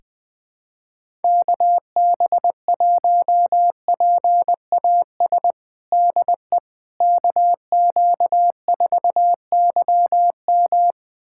Morse code, heard as a keyed tone, about 21 wpm, 700 Hz. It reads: KB1PAS DE KQ4YM
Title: USA Callsign DE Callsign 20wpm